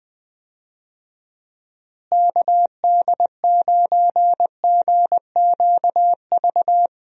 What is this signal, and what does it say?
Morse code, heard as a keyed tone, about 21 wpm, 700 Hz. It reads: KD9GQV
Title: USA Call Signs - 6 Characters - Repeated 3x - 20wpm